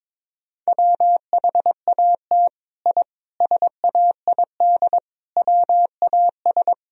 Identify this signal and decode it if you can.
Morse code, heard as a keyed tone, about 21 wpm, 700 Hz. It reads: WHAT I SAID WAS
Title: Sentences from Top 1000 Words - Review Part 4 - 22wpm